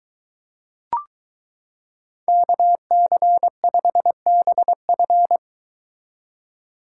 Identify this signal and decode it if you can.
Morse code, heard as a keyed tone, about 22 wpm, 700 Hz. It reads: KC5BF